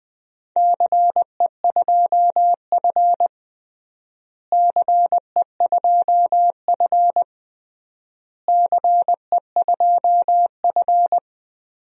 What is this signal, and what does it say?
Morse code, heard as a keyed tone, about 20 wpm, 700 Hz. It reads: CE2F CE2F CE2F
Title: DXCC-100 Call Signs - 4 Characters - Repeated 3x - 20wpm